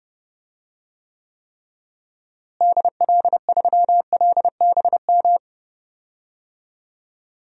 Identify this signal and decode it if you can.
Morse code, heard as a keyed tone, about 30 wpm, 700 Hz. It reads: DL3LBM
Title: DXCC-100 Call Signs - 6 Characters - Repeated 3x - 30wpm